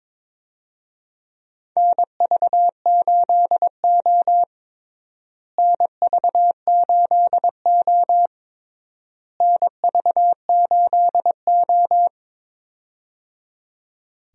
Morse code, heard as a keyed tone, about 22 wpm, 700 Hz. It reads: NV8O NV8O NV8O
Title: USA Call Signs - 4 Characters - Repeated 3x - 22wpm